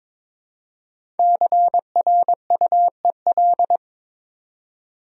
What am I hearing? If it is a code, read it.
Morse code, heard as a keyed tone, about 22 wpm, 690 Hz. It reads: CRUEL
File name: 5 Letter Words - First 3 Letters Given - 22wpm